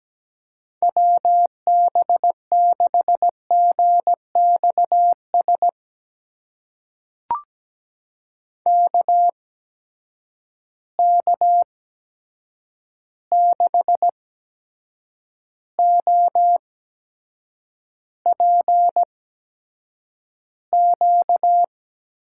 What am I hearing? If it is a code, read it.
Morse code, heard as a keyed tone, about 17 wpm, spaced as for 7 wpm, 700 Hz. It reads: WB6GXS KK6OPQ
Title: USA Call Signs - 6 Characters - 8x Character Spacing - 17wpm